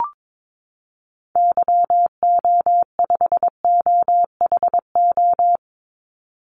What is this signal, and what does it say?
Morse code, heard as a keyed tone, about 22 wpm, 700 Hz. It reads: YO5OHO
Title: Call Signs Repeated 3x - 22wpm